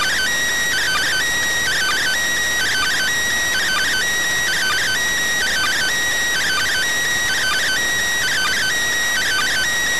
Two-tone teleprinter signal, about 400 Hz shift.